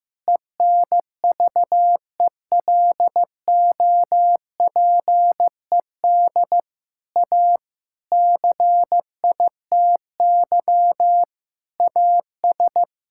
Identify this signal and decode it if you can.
Morse code, heard as a keyed tone, about 16 wpm, 700 Hz. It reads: ENVELOPED A CITY AS
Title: Pangram Challenge v2 - 15wpm